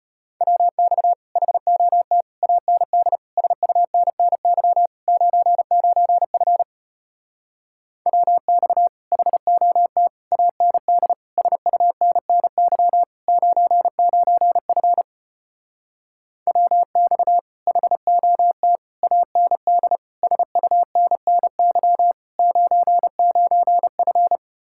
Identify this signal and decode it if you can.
Morse code, heard as a keyed tone, about 33 wpm, 700 Hz. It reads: WX HOT AND SUNNY 99F WX HOT AND SUNNY 99F WX HOT AND SUNNY 99F